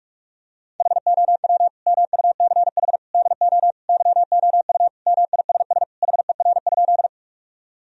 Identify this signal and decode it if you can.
Morse code, heard as a keyed tone, about 46 wpm, 700 Hz. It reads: HOW MUCH DO YOU MISS HER?
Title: Sentences from Top 1000 Words - Review Part 3 - 45wpm